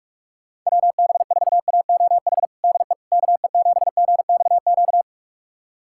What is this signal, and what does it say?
Morse code, heard as a keyed tone, about 46 wpm, 700 Hz. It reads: WB4AOH DE KE7GXQ
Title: USA Callsign DE Callsign 45wpm